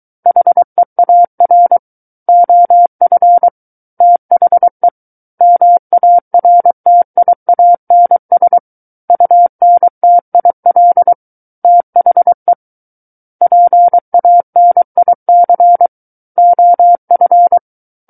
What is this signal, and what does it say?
Morse code, heard as a keyed tone, about 23 wpm, 700 Hz. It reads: HEAR OF THE MARTIANS UNTIL THE PANIC OF
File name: Book One / 14-Chapter 14 - 23 WPM